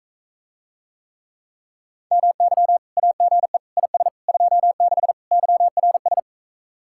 Morse code, heard as a keyed tone, about 45 wpm, 700 Hz. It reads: MY AGE IS 26 YRS